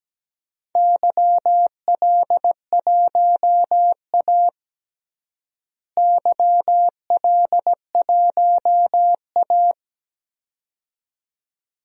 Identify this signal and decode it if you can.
Morse code, heard as a keyed tone, about 17 wpm, 700 Hz. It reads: YL1A YL1A
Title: DXCC-100 Call Signs - 4 Characters - Repeated 2x - 17wpm